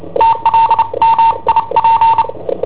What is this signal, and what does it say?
Morse code, heard as a keyed tone, about 29 wpm, 910 Hz. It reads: TLMIP
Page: Cassetta esercizi preparata da I7OHP per IW5ELP